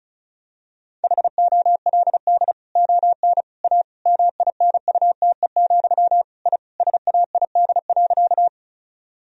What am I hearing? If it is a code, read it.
Morse code, heard as a keyed tone, about 36 wpm, 700 Hz. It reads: HOLD ON A MINUTE, I SAID.
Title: Sentences from Top 1000 Words - Review Part 2 - 35wpm